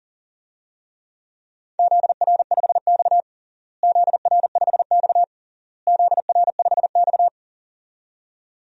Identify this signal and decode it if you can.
Morse code, heard as a keyed tone, about 40 wpm, 700 Hz. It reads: ZR5X ZR5X ZR5X
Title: DXCC-100 Call Signs - 4 Characters - Repeated 3x - 40wpm